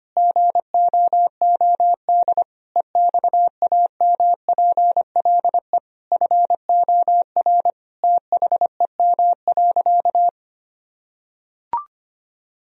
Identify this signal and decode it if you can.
Morse code, heard as a keyed tone, about 25 wpm, 700 Hz. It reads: GOOD EXAMPLE FOR THEM.
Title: Sentences from Top 1000 Words - Review Part 1 - 25wpm